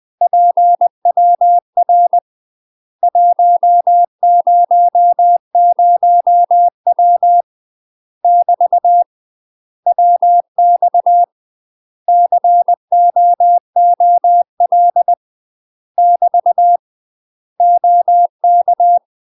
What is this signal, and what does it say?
Morse code, heard as a keyed tone, about 20 wpm, 700 Hz. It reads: PWR 100W <BT> WX COOL <BT> OK
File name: UK MASTER QSOs TRAINING / 20WPM / UK MASTER QSOs 0071